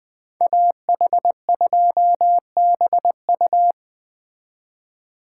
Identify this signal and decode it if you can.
Morse code, heard as a keyed tone, about 20 wpm, 700 Hz. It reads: AH2BU
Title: Speed Racing - USA 2x2 Call Signs 17wpm